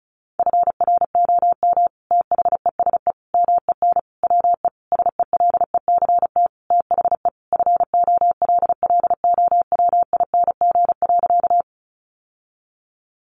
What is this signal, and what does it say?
Morse code, heard as a keyed tone, about 34 wpm, 700 Hz. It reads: FROM THESE MEN WE SELECT THE FOLLOWING.
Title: Sentences from Top 1300 Words 35wpm